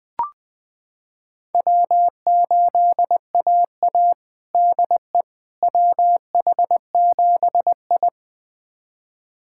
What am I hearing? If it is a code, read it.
Morse code, heard as a keyed tone, about 20 wpm, 700 Hz. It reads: W8AA DE WH7I
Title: USA Callsign DE Callsign - 4 Characters - 20wpm